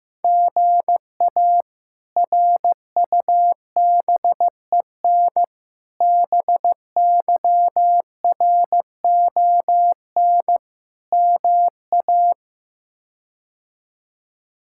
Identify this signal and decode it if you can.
Morse code, heard as a keyed tone, about 15 wpm, 700 Hz. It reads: GA RUBEN BYRON MA
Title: K1USN SST QSO Practice - 15wpm